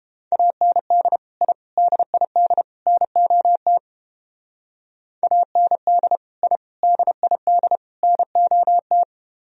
Morse code, heard as a keyed tone, about 30 wpm, 700 Hz. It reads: AND I DID NOT AND I DID NOT